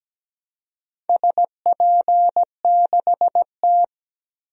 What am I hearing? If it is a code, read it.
Morse code, heard as a keyed tone, about 17 wpm, 700 Hz. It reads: SP6T
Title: DXCC-100 Call Signs - 4 Characters - 17wpm